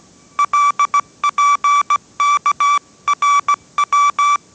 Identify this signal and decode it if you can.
Morse code, heard as a keyed tone, about 17 wpm, 1160 Hz. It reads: LPKRW
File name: 18wpm